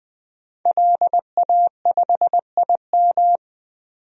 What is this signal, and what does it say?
Morse code, heard as a keyed tone, about 20 wpm, 700 Hz. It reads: LA5IM
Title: DXCC-100 Call Signs - 5 Characters - 20wpm